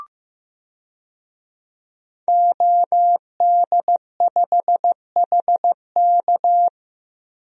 Morse code, heard as a keyed tone, about 15 wpm, 700 Hz. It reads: OD5HK
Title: DXCC-100 Call Signs - 5 Characters - Repeated 3x - 15wpm